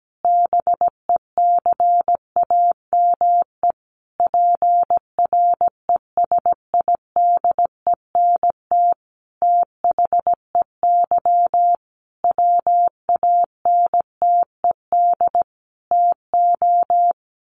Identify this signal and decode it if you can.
Morse code, heard as a keyed tone, about 17 wpm, 700 Hz. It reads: BECAME PRESIDENT THEY WANTED TO
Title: A Sound Of Thunder 17wpm